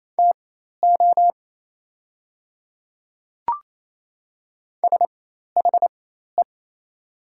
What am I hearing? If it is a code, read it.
Morse code, heard as a keyed tone, about 28 wpm, spaced as for 13 wpm, 700 Hz. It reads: TO SHE
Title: 4x Character Spacing - Top 100 Words - 28wpm